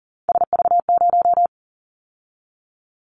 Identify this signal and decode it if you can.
Morse code, heard as a keyed tone, about 40 wpm, 700 Hz. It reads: SV0
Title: Callsign Prefixes for Top 50 DXCC - 40wpm